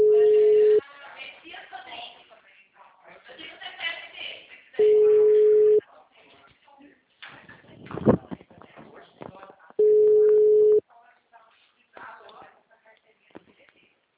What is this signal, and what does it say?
Ringback tone (Europe, 425 Hz), with speech.